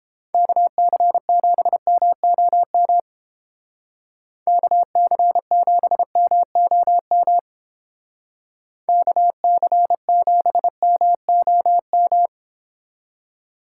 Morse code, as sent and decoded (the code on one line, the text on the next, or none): -.- -.-. --... -- --- -- / -.- -.-. --... -- --- -- / -.- -.-. --... -- --- --
KC7MOM KC7MOM KC7MOM